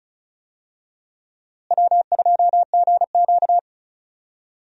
Morse code, heard as a keyed tone, about 34 wpm, 700 Hz. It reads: W2GQ
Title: USA Call Signs - 4 Characters - Repeated 3x - 35wpm